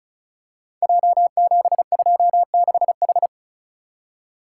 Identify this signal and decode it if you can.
Morse code, heard as a keyed tone, about 36 wpm, 700 Hz. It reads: J726H
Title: Code Groups - 5 Letter:Numbers - 35wpm